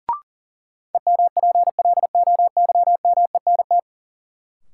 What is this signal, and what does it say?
Morse code, heard as a keyed tone, about 40 wpm, 700 Hz. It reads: EMPLOYMENT
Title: Sentences from Top 1500 Words - New Word Drill Alt - 40wpm